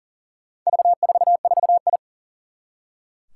Morse code, heard as a keyed tone, about 40 wpm, 700 Hz. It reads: V44I
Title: Four Challenging Letter-Number Combinations 40wpm